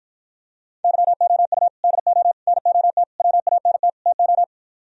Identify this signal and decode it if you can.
Morse code, heard as a keyed tone, about 52 wpm, 700 Hz. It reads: YOU DO NOT WANT TO